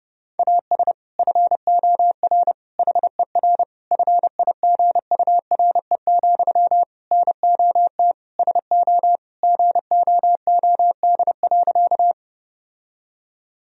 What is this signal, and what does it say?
Morse code, heard as a keyed tone, about 29 wpm, 700 Hz. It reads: AS FOR HER FIGURE, NOT SO GOOD.
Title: Sentences from Top 1000 Words - Review Part 2 - 30wpm